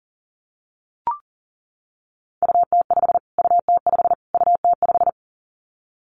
Morse code, heard as a keyed tone, about 40 wpm, 700 Hz. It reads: UT5 UT5 UT5
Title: Callsign Prefixes for Top 25 DXCC - Repeated 3x - 40wpm